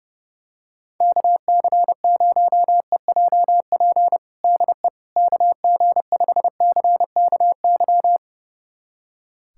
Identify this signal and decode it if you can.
Morse code, heard as a keyed tone, about 29 wpm, 700 Hz. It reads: KC0EJP DE KG5CKY
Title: USA Callsign DE Callsign - 6 Characters - 30wpm